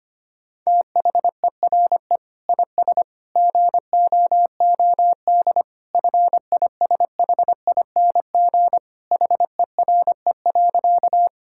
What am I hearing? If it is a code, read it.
Morse code, heard as a keyed tone, about 25 wpm, 700 Hz. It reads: THERE IS GOOD FISHING HERE.
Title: Sentences from Top 1000 Words - Review Part 5 - 25wpm